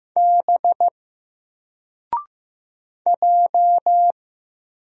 Morse code, heard as a keyed tone, about 15 wpm, 700 Hz. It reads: B J
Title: Single Character (Letter:Number:Punctuation:Prosign) Alt 15wpm